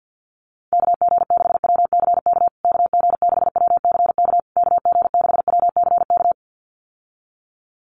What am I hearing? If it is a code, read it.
Morse code, heard as a keyed tone, about 50 wpm, 700 Hz. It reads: KG6WCK KG6WCK KG6WCK
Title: USA Call Signs Repeated 3x 50wpm